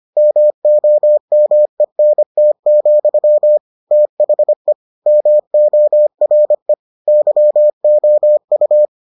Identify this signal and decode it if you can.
Morse code, heard as a keyed tone, about 25 wpm, 600 Hz. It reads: MOMENT, THE MORE YOU